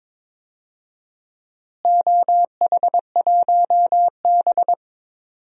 Morse code, heard as a keyed tone, about 22 wpm, 700 Hz. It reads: OH1B
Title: DXCC-100 Call Signs - 4 Characters - Repeated 3x - 22wpm